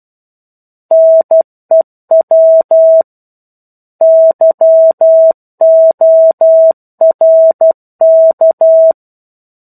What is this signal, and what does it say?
Morse code, heard as a keyed tone, about 12 wpm, 650 Hz. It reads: NEW YORK
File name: 01-Chapter-01 12 WPM